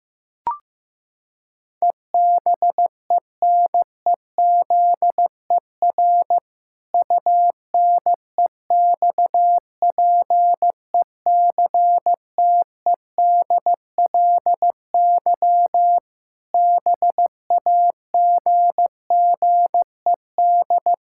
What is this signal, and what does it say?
Morse code, heard as a keyed tone, about 16 wpm, 700 Hz. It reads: EBENEZER UNEXPECTEDLY BAGGED
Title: Pangram Challenge v2 - 15wpm